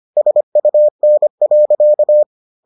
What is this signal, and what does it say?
Morse code, heard as a keyed tone, about 25 wpm, 600 Hz. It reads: SUN.